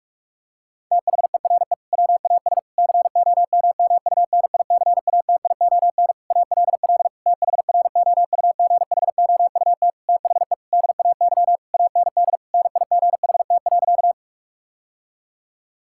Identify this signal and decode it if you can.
Morse code, heard as a keyed tone, about 41 wpm, 700 Hz. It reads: THERE WAS COMMUNICATION ALL THROUGHOUT THE DAY AND NIGHT.
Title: Sentences from Top 1000 Words - Review Part 6 - 45wpm